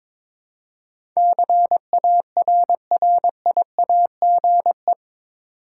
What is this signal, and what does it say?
Morse code, heard as a keyed tone, about 22 wpm, 700 Hz. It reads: CARRIAGE
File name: practice / 22wpm / Eight Letter Words - 22wpm